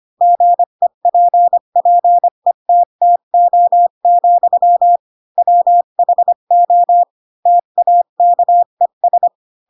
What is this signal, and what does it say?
Morse code, heard as a keyed tone, about 25 wpm, 700 Hz. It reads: GEPPETTO, WHO TAKES
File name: Pinocchio / Pinocchio ch2 20 25 ik1hge